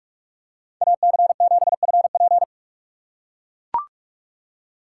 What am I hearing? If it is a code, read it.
Morse code, heard as a keyed tone, about 43 wpm, 700 Hz. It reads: AC7FP
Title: USA Call Signs - 5 Characters - 3x Character Spacing - 45wpm